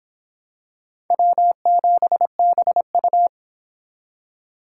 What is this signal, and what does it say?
Morse code, heard as a keyed tone, about 26 wpm, 700 Hz. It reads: W7BU